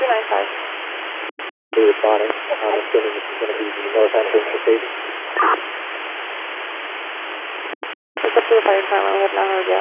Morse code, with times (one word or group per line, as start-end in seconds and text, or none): none